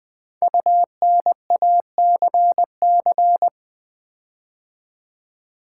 Morse code, heard as a keyed tone, about 20 wpm, 700 Hz. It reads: UNACC
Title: Build A Word - 5 Letters At A Time - 12-Letter Words - 20wpm